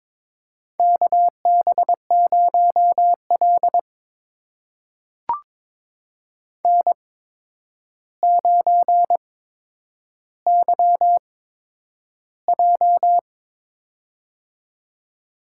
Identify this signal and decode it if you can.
Morse code, heard as a keyed tone, about 22 wpm, spaced as for 9 wpm, 700 Hz. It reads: KB0L N9YJ
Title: USA Call Signs - 4 Characters - 8x Character Spacing - 22wpm